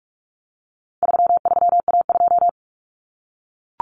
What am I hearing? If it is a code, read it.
Morse code, heard as a keyed tone, about 44 wpm, 700 Hz. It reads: 33A2
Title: Common Cut Numbers - 4-Digit 45wpm